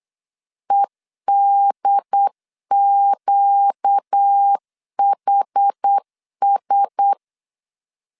Morse code, heard as a keyed tone, about 8 wpm, 800 Hz. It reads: EDQHS
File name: LEZIONI / 6 PKX0 / 6 P 8wpm 3025